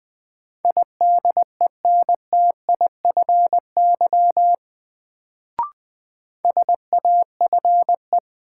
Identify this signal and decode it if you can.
Morse code, heard as a keyed tone, about 21 wpm, 700 Hz. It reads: IDENTIFY SAFE
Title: Sentences from Top 900 Words - New Word Drill - 20wpm